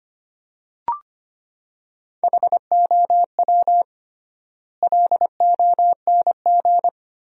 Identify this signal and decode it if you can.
Morse code, heard as a keyed tone, about 25 wpm, 700 Hz. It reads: HOW LONG
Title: 3x Word-Spacing - Sentences from Top 500 Words - 25wpm